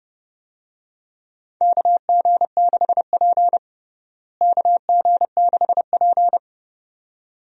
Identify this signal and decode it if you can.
Morse code, heard as a keyed tone, about 30 wpm, 700 Hz. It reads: KG6P KG6P
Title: USA Call Signs - 4 Characters - Repeated 2x - 30wpm